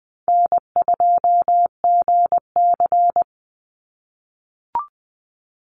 Morse code, heard as a keyed tone, about 20 wpm, 700 Hz. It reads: N2GC
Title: Easy Call Signs - 20wpm